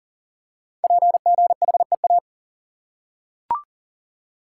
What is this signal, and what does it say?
Morse code, heard as a keyed tone, about 42 wpm, 690 Hz. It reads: PGHEA
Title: Code Groups - 5 Letters - 40wpm